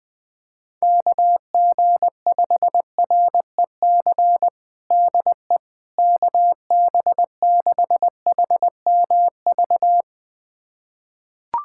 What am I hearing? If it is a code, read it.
Morse code, heard as a keyed tone, about 20 wpm, 700 Hz. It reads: KG5REC DE KB6HMV
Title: USA Callsign DE Callsign - 6 Characters - 20wpm